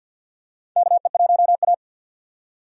Morse code, heard as a keyed tone, about 50 wpm, 700 Hz. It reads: KE1A
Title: USA Call Signs - 4 Characters - 50wpm